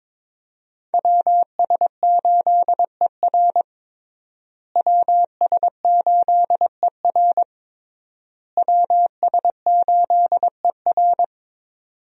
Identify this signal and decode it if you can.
Morse code, heard as a keyed tone, about 22 wpm, 700 Hz. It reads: WS8ER WS8ER WS8ER
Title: USA Call Signs - 5 Characters - Repeated 3x - 22wpm